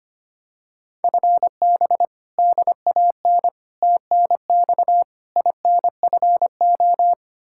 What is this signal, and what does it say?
Morse code, heard as a keyed tone, about 25 wpm, 700 Hz. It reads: FB DAN TNX INFO
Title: Practice QSO - Exchange 3 - More Info - 25wpm